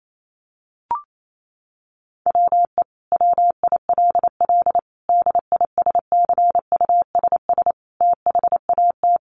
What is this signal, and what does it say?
Morse code, heard as a keyed tone, about 28 wpm, 700 Hz. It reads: WE WILL DISCUSS THAT
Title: Sentences from Top 1000 Words 28wpm